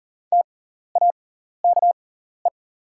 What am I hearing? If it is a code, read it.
Morse code, heard as a keyed tone, about 40 wpm, spaced as for 14 wpm, 690 Hz. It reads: TAKE
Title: 6x Character Spacing - Top 300 Words - 40wpm